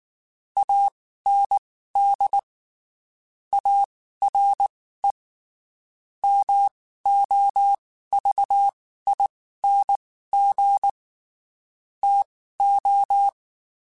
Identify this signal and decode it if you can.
Morse code, heard as a keyed tone, about 19 wpm, spaced as for 14 wpm, 800 Hz. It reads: AND ARE MOVING TO
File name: (spaced) / QSOs Spaced-mp3s / WD0BGP 1 @18 sp